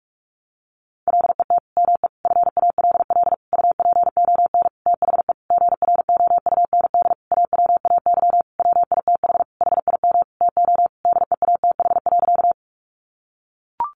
Morse code, heard as a keyed tone, about 44 wpm, 700 Hz. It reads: LET ME FALL UPON THE GROUND AWAY WITH HIM TO DEATH.
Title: Sentences from Top 1100 Words 45wpm